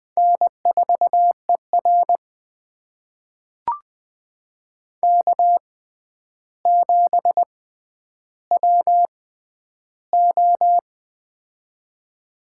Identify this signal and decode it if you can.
Morse code, heard as a keyed tone, about 20 wpm, spaced as for 10 wpm, 700 Hz. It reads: N4ER K7WO
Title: USA Call Signs - 4 Characters - 6x Character Spacing - 20wpm